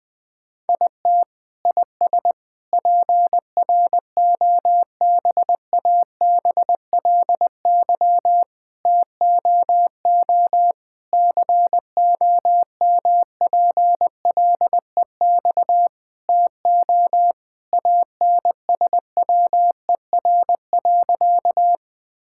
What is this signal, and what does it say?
Morse code, heard as a keyed tone, about 20 wpm, 700 Hz. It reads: IT IS PROBABLY TOO COMPLEX TO ANSWER.
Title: Sentences from Top 1000 Words - Review Part 6 - 20wpm